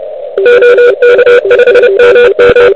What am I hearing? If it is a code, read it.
Morse code, heard as a keyed tone, about 34 wpm, 500 Hz, with noise. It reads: OK5MM
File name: ARCHIV / QCX / QCX Pileup